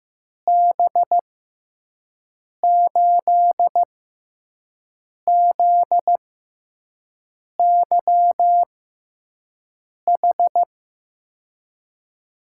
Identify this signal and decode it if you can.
Morse code, heard as a keyed tone, about 15 wpm, spaced as for 5 wpm, 700 Hz. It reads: B8ZYH
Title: USA Call Signs - 6 Characters - 6x Character Spacing - 15wpm